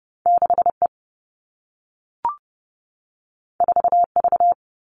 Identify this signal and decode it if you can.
Morse code, heard as a keyed tone, about 30 wpm, 700 Hz. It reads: 6E 4V
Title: All Cut Numbers - 2-Digit 30wpm